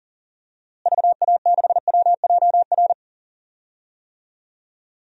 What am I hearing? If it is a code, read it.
Morse code, heard as a keyed tone, about 38 wpm, 700 Hz. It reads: VA6WJR
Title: DXCC-100 Call Signs - 6 Characters - Repeated 2x - 40wpm